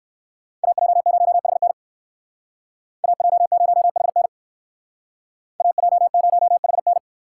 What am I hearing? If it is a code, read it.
Morse code, heard as a keyed tone, about 59 wpm, 700 Hz. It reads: AJ0HN AJ0HN AJ0HN